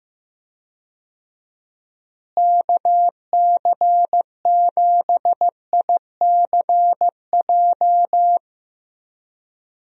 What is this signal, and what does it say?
Morse code, heard as a keyed tone, about 15 wpm, 700 Hz. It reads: KC7ICJ
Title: USA Call Signs - 6 Characters - Repeated 3x - 15wpm